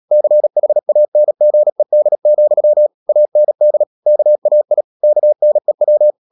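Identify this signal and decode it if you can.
Morse code, heard as a keyed tone, about 37 wpm, 600 Hz. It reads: CHANGED, AND KAI KNEW
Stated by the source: Morse practice at 36 wpm, text of Ken Follett's novel Never, from CW med Gnister.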